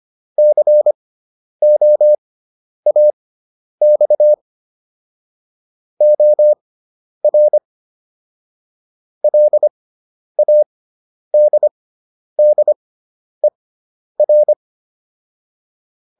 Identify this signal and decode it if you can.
Morse code, heard as a keyed tone, about 25 wpm, spaced as for 10 wpm, 600 Hz. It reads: COAX OR LADDER